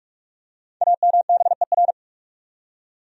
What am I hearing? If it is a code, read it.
Morse code, heard as a keyed tone, about 46 wpm, 700 Hz. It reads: AMBER
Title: USA Common Names - Max 5 Letters - 45wpm